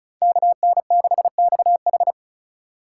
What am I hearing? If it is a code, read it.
Morse code, heard as a keyed tone, about 46 wpm, 700 Hz. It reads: KN6XH